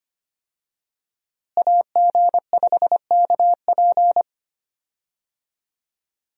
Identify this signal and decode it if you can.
Morse code, heard as a keyed tone, about 25 wpm, 700 Hz. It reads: AG5KP
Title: USA Call Signs - 5 Characters - Repeated 2x - 25wpm